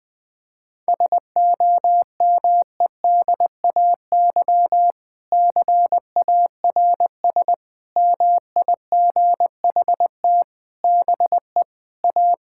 Morse code, heard as a keyed tone, about 20 wpm, 700 Hz. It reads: SOMEDAY CARS MIGHT BE A
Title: Sentences with 6-Word Phrases - 20wpm